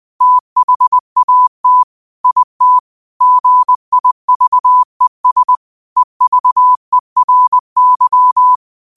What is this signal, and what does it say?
Morse code, heard as a keyed tone, about 20 wpm, 1000 Hz. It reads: THAT IT GIVES EVERY
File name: qotdfiles / 20wpm